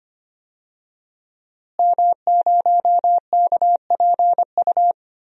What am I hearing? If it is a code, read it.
Morse code, heard as a keyed tone, about 25 wpm, 700 Hz. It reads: M0KPU
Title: DXCC-100 Call Signs - 5 Characters - Repeated 3x - 25wpm